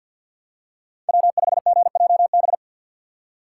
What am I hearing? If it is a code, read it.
Morse code, heard as a keyed tone, about 48 wpm, 700 Hz. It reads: W5GJB
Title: Speed Racing - USA 1x3 Call Signs 50wpm